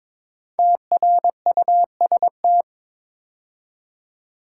Morse code, heard as a keyed tone, about 22 wpm, 700 Hz. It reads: TRUST
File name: 5 Letter Words - First 3 Letters Given - 22wpm